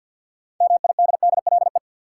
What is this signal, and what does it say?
Morse code, heard as a keyed tone, about 51 wpm, 700 Hz. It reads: MIDDLE
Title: Sets of 2 Words - New Word Drill - 50wpm